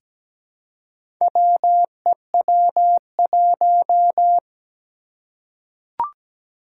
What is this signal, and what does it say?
Morse code, heard as a keyed tone, about 17 wpm, 700 Hz. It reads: WEW1
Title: Four Challenging Letter-Number Combinations 17wpm